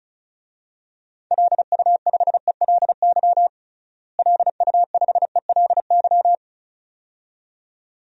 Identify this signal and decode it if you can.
Morse code, heard as a keyed tone, about 34 wpm, 700 Hz. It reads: LU5ELY LU5ELY
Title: DXCC-100 Call Signs - 6 Characters - Repeated 2x - 35wpm